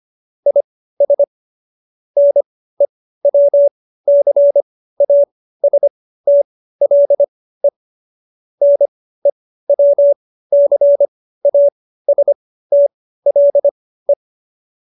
Morse code, heard as a keyed tone, about 25 wpm, spaced as for 15 wpm, 570 Hz. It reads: IS NEWCASTLE NEWCASTLE